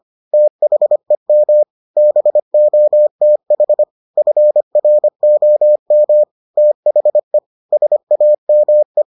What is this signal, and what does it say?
Morse code, heard as a keyed tone, about 25 wpm, 600 Hz. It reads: THEM BOTH FROM THE SAME